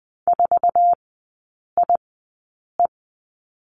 Morse code, heard as a keyed tone, about 20 wpm, spaced as for 8 wpm, 700 Hz. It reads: 4IE
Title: 067 Lesson 12e - Letter d - Three Characters 20wpm